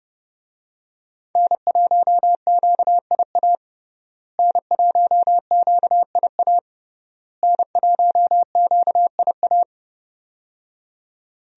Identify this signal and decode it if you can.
Morse code, heard as a keyed tone, about 30 wpm, 700 Hz. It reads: N1QIA N1QIA N1QIA
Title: USA Call Signs - 5 Characters - Repeated 3x - 30wpm